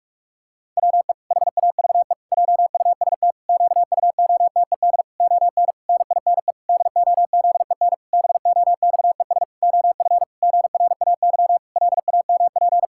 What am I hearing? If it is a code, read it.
Morse code, heard as a keyed tone, about 46 wpm, 700 Hz. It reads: WE HAVE JUST QUOTED ON NINE DOZEN BOXES OF GRAY LAMP
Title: Pangram Challenge v2 - 45wpm